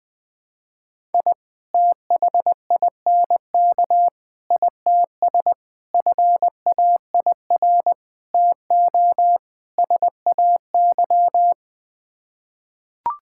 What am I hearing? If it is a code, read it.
Morse code, heard as a keyed tone, about 20 wpm, 700 Hz. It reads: I THINK ITS FAIR TO SAY
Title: Sets of 6 Words - 20wpm